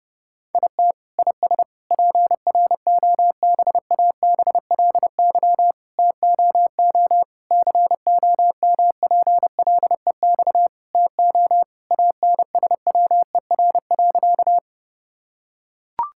Morse code, heard as a keyed tone, about 32 wpm, 700 Hz. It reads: IT IS PROBABLY TOO COMPLEX TO ANSWER.
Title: Sentences from Top 1000 Words - Review Part 6 - 30wpm